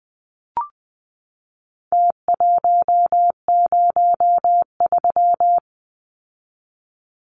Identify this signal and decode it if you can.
Morse code, heard as a keyed tone, about 20 wpm, 700 Hz. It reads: T103
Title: Numbers - Contest Style - 4-Digit 20wpm